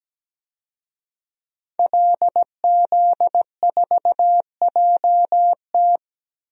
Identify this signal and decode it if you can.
Morse code, heard as a keyed tone, about 17 wpm, 700 Hz. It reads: LZ4JT
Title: DXCC-100 Call Signs - 5 Characters - 17wpm